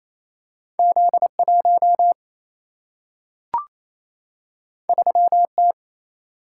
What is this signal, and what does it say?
Morse code, heard as a keyed tone, about 28 wpm, 700 Hz. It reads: Z1 3T
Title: Single-Letter Single-Number 28wpm